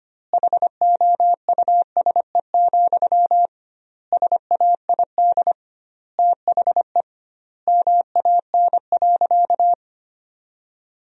Morse code, heard as a keyed tone, about 25 wpm, 700 Hz. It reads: HOUSE, SAID THE MAN.
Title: 2x Word-Spacing - Sentences from Top 400 Words - 25wpm